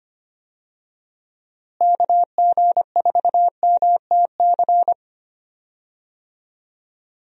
Morse code, heard as a keyed tone, about 25 wpm, 700 Hz. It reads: KG4MTC
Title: USA Call Signs - 6 Characters - Repeated 3x - 25wpm